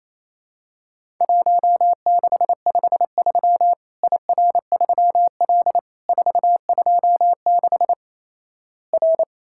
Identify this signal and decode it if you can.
Morse code, heard as a keyed tone, about 28 wpm, 700 Hz. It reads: 1653 IR3L 426 R
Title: Worked All Europe DX Contest - QTC Exchange - 28wpm